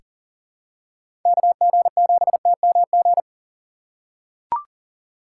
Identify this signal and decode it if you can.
Morse code, heard as a keyed tone, about 40 wpm, 700 Hz. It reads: KG7TMG
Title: USA Call Signs - 3x Character Spacing - 40wpm